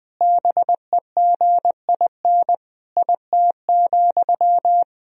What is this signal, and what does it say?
Morse code, heard as a keyed tone, about 20 wpm, 700 Hz. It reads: BEGIN IT,
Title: Sentences from Top 1000 Words - Review Part 1 - 20wpm